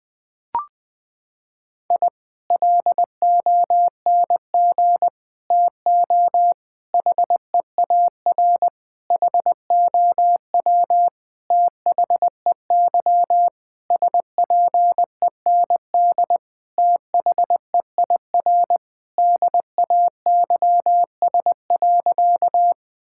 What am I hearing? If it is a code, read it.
Morse code, heard as a keyed tone, about 20 wpm, 700 Hz. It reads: I LONG TO HEAR HOW THEY SPEND THEIR DAYS.
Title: Sentences from Top 1000 Words - Review Part 5 - 20wpm